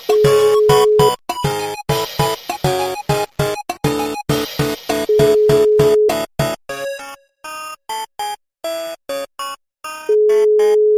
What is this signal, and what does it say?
Ringback tone (Europe, 425 Hz).